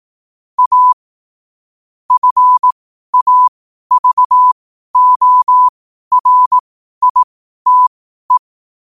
Morse code, heard as a keyed tone, about 18 wpm, spaced as for 12 wpm, 1000 Hz. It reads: A FAVORITE
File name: qotdfiles / 12wpm